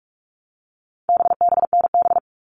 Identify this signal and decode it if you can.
Morse code, heard as a keyed tone, about 44 wpm, 700 Hz. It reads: BBNB
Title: All Cut Numbers - 4-Digit 45wpm